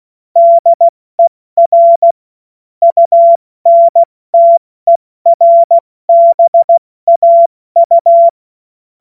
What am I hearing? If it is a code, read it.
Morse code, heard as a keyed tone, about 16 wpm, 680 Hz. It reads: DER UNTERBAU
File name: Morsekurs / Witze / Witz 21. Juni 2022 70 F80